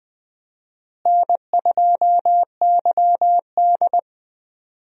Morse code, heard as a keyed tone, about 20 wpm, 700 Hz. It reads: N2YD
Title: USA Call Signs - 4 Characters - Repeated 3x - 20wpm